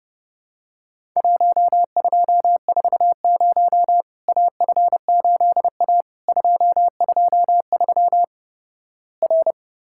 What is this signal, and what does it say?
Morse code, heard as a keyed tone, about 30 wpm, 700 Hz. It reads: 1240 AF8A 223 R
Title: Worked All Europe DX Contest - QTC Exchange - 30wpm